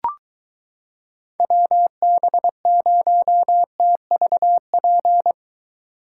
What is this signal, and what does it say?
Morse code, heard as a keyed tone, about 23 wpm, 700 Hz. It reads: WB0TVP